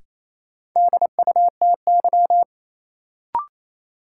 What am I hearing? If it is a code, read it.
Morse code, heard as a keyed tone, about 28 wpm, 700 Hz. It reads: DUTY
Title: Sentences from Top 1500 Words - New Word Drill - 28wpm